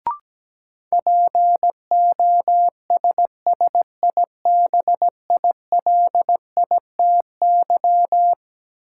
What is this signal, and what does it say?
Morse code, heard as a keyed tone, about 17 wpm, 700 Hz. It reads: POSSIBILITY
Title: Sets of 3 Words - New Word Drill - 17wpm